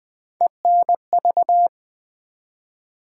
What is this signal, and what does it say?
Morse code, heard as a keyed tone, about 20 wpm, 700 Hz. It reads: ENV
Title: Build A Word - 3 Letters At A Time - 8-Letter Words - 20wpm